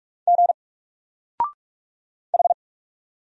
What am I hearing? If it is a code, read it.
Morse code, heard as a keyed tone, about 44 wpm, 700 Hz. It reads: G H
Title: Single Letter-Number 45wpm